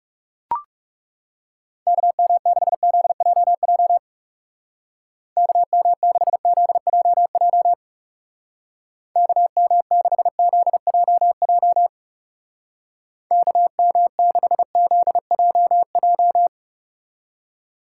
Morse code, heard as a keyed tone, about 36 wpm, 700 Hz. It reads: KM6ZJJ KM6ZJJ KM6ZJJ KM6ZJJ
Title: Speed Racing - USA 2x3 Call Signs 30wpm